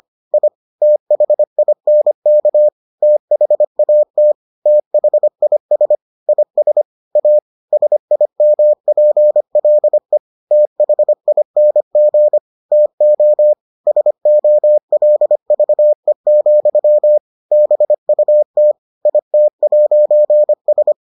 Morse code, heard as a keyed tone, about 25 wpm, 600 Hz. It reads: I THINK THAT THIS IS A SIMPLE THING TO SOLVE, BUT IT'S